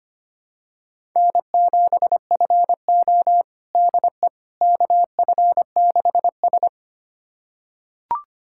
Morse code, heard as a keyed tone, about 25 wpm, 700 Hz. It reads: N7FO DE KF6S
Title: USA Callsign DE Callsign - 4 Characters - 25wpm